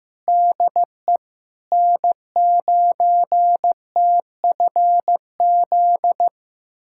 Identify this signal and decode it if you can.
Morse code, heard as a keyed tone, about 15 wpm, 700 Hz. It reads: DE N9TFZ
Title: USA Callsign DE Callsign - 5 Characters - 15wpm